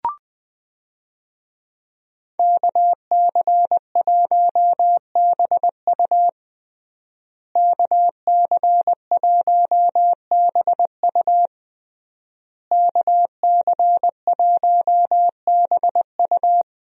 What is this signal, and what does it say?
Morse code, heard as a keyed tone, about 20 wpm, 700 Hz. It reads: KC1BU KC1BU KC1BU
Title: USA Call Signs - 5 Characters - Repeated 3x - 20wpm